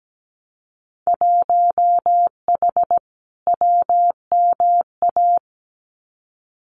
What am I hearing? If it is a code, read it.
Morse code, heard as a keyed tone, about 17 wpm, 700 Hz. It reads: 1H WMA
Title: Winter Field Day Exchanges 17wpm